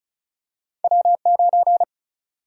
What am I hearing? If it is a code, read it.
Morse code, heard as a keyed tone, about 36 wpm, 700 Hz. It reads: W9
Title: USA Call Sign Prefixes - 35wpm